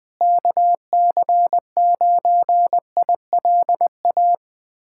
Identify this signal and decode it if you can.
Morse code, heard as a keyed tone, about 20 wpm, 700 Hz. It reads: KC9ILA
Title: USA Callsign DE Callsign 20wpm